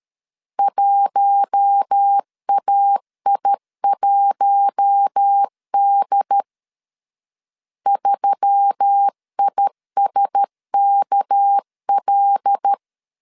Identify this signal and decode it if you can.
Morse code, heard as a keyed tone, about 13 wpm, 800 Hz. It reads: 1AI1D 3ISKL
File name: CW / DET / 1 6 1500C S B 12wpm 3431